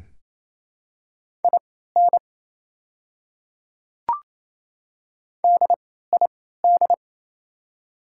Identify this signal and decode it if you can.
Morse code, heard as a keyed tone, about 27 wpm, spaced as for 16 wpm, 700 Hz. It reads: IN DID
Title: 3x Character Spacing - Top 100 Words - 28wpm